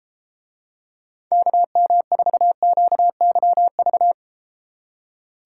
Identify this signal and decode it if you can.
Morse code, heard as a keyed tone, about 32 wpm, 700 Hz. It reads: KM4QYV